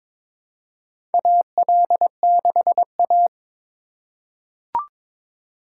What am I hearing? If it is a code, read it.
Morse code, heard as a keyed tone, about 21 wpm, 700 Hz. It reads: AL6A
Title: USA Call Signs - 4 Characters - 2x Character Spacing - 22wpm